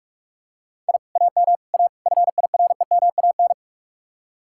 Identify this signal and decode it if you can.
Morse code, heard as a keyed tone, about 49 wpm, 700 Hz. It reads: I AM A FIREMAN